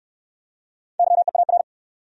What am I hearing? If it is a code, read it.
Morse code, heard as a keyed tone, about 68 wpm, 700 Hz. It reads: YEAR